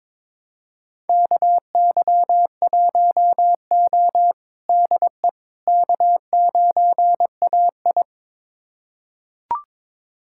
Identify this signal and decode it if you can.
Morse code, heard as a keyed tone, about 22 wpm, 700 Hz. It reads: KY1O DE K9AI
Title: USA Callsign DE Callsign - 4 Characters - 22wpm